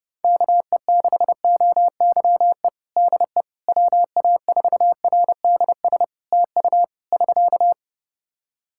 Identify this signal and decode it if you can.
Morse code, heard as a keyed tone, about 32 wpm, 700 Hz. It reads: KE6OYE DE WA4RDS TU <SK>
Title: Practice QSO - Exchange 4 - Ending - 30wpm